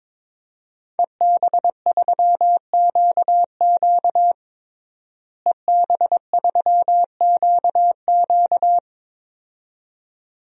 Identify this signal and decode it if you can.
Morse code, heard as a keyed tone, about 21 wpm, 700 Hz. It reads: EB3QQ EB3QQ
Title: DXCC-100 Call Signs - 5 Characters - Repeated 2x - 22wpm